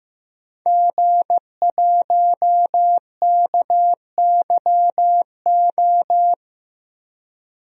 Morse code, heard as a keyed tone, about 15 wpm, 700 Hz. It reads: G1KYO
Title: DXCC-100 Call Signs - 5 Characters - 15wpm